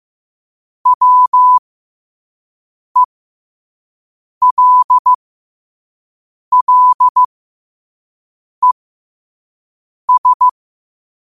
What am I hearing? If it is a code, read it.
Morse code, heard as a keyed tone, about 15 wpm, spaced as for 5 wpm, 1000 Hz. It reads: WELLES
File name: qotdfiles / 5wpm